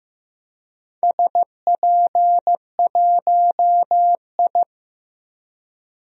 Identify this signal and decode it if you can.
Morse code, heard as a keyed tone, about 15 wpm, 700 Hz. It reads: SP1I
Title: DXCC-100 Call Signs - 4 Characters - Repeated 2x - 15wpm